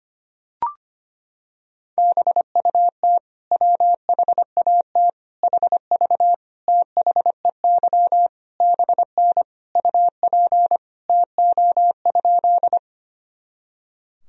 Morse code, heard as a keyed tone, about 25 wpm, 700 Hz. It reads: BUT WHAT HV THEY BN UP TO?
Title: Sentences from Top 100 Words and Abbreviations 25wpm